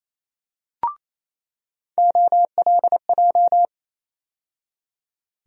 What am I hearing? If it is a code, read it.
Morse code, heard as a keyed tone, about 28 wpm, 700 Hz. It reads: OLJ
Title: USA Call Sign Postfixes - 3 Letters 28wpm